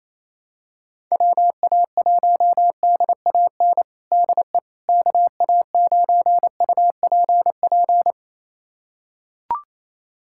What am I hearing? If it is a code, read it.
Morse code, heard as a keyed tone, about 28 wpm, 700 Hz. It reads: WA1DAN DE KA9UPP
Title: USA Callsign DE Callsign 28wpm